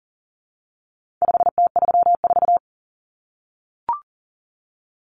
Morse code, heard as a keyed tone, about 40 wpm, 700 Hz. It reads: 5T34
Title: Common Cut Numbers - 4-Digit 40wpm